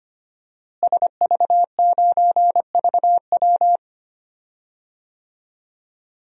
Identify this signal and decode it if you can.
Morse code, heard as a keyed tone, about 25 wpm, 700 Hz. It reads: SV9VW
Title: DXCC-100 Call Signs - 5 Characters - Repeated 3x - 25wpm